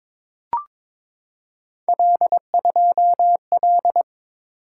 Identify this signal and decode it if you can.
Morse code, heard as a keyed tone, about 22 wpm, 700 Hz. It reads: L2L
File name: practice / 22wpm / Three Letter-Number Combinations - 22wpm